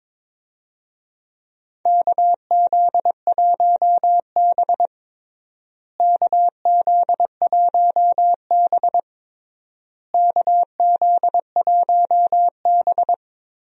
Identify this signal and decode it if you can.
Morse code, heard as a keyed tone, about 22 wpm, 700 Hz. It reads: KZ1B KZ1B KZ1B
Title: USA Call Signs - 4 Characters - Repeated 3x - 22wpm